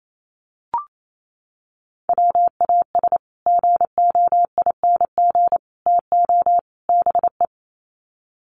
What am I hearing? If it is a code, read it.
Morse code, heard as a keyed tone, about 28 wpm, 700 Hz. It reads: WAS GOING TO BE
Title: Sets of 4 Words 28wpm